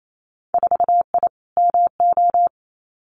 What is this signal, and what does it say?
Morse code, heard as a keyed tone, about 28 wpm, 700 Hz. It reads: 4I MO
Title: Winter Field Day Exchanges 28wpm